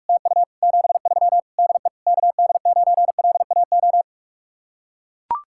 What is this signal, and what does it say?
Morse code, heard as a keyed tone, about 44 wpm, 700 Hz. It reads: TU 73 DE KD9LAO